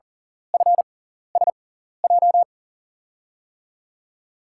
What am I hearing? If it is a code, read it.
Morse code, heard as a keyed tone, about 40 wpm, spaced as for 14 wpm, 700 Hz. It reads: FSJ
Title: USA Call Signs - 6 Characters - 6x Character Spacing - 40wpm